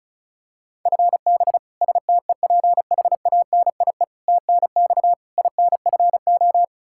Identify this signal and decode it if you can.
Morse code, heard as a keyed tone, about 36 wpm, 700 Hz. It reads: FB STEPHANIE TNX INFO
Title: Practice QSO - Exchange 3 - More Info - 35wpm